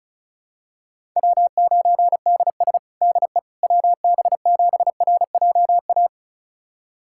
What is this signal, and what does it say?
Morse code, heard as a keyed tone, about 36 wpm, 700 Hz. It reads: W9DS DE WB7RJA
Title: USA Callsign DE Callsign 35wpm